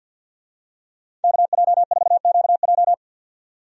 Morse code, heard as a keyed tone, about 51 wpm, 700 Hz. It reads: KJ4QJ
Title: USA Call Signs - 5 Characters - 50wpm